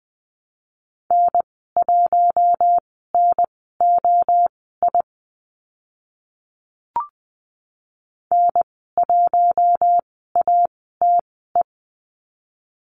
Morse code, heard as a keyed tone, about 20 wpm, spaced as for 14 wpm, 700 Hz. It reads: N1NOI N1ATE
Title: 042 Lesson 8g - Number 1 - Call Signs 20wpm